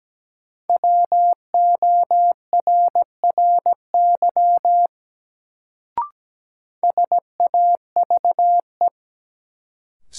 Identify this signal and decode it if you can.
Morse code, heard as a keyed tone, about 17 wpm, 700 Hz. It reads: WORRY SAVE
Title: Sentences from Top 1100 Words - New Word Drill Alt - 17wpm